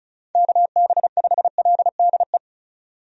Morse code, heard as a keyed tone, about 36 wpm, 700 Hz. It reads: KB5LDE
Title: USA Call Signs - 6 Characters - 35wpm